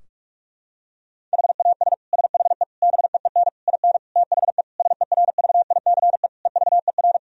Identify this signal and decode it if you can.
Morse code, heard as a keyed tone, about 46 wpm, 700 Hz. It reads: HAS SHE BEEN IN THE SERVICE EVER
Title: Sentences from Top 1000 Words - Review Part 4 - 45wpm